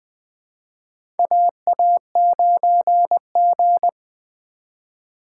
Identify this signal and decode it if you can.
Morse code, heard as a keyed tone, about 20 wpm, 700 Hz. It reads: AA9G
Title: USA Call Signs - 4 Characters - 4x Character Spacing - 20wpm